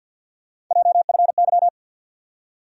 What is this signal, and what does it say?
Morse code, heard as a keyed tone, about 52 wpm, 700 Hz. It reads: JFY
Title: USA Call Sign Postfixes - 3 Letters 50wpm